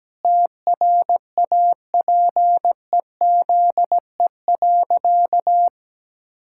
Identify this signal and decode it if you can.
Morse code, heard as a keyed tone, about 17 wpm, 700 Hz. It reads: TRAPEZE.
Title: Pangram Challenge v2 - 17wpm